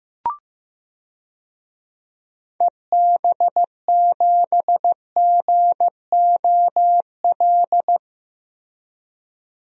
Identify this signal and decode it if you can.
Morse code, heard as a keyed tone, about 15 wpm, 700 Hz. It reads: EB7GOL
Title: DXCC-100 Call Signs - 6 Characters - 15wpm